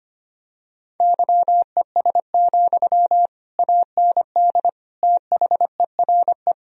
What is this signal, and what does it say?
Morse code, heard as a keyed tone, about 25 wpm, 700 Hz. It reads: YES, AND THERE
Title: Sentences from Top 1000 Words - Review Part 3 - 25wpm